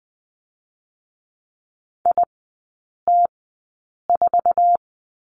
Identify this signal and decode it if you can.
Morse code, heard as a keyed tone, about 20 wpm, spaced as for 8 wpm, 700 Hz. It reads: IT4
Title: 053 Lesson 10e - Letter r - Three Characters 20wpm